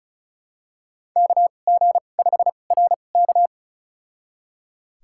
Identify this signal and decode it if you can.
Morse code, heard as a keyed tone, about 34 wpm, spaced as for 25 wpm, 700 Hz. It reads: KG5RK
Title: USA Call Signs - 2x Character Spacing - 35wpm